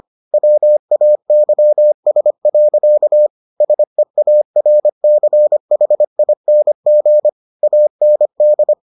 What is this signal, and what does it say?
Morse code, heard as a keyed tone, about 25 wpm, 600 Hz. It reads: WAYS. SEARCHING AND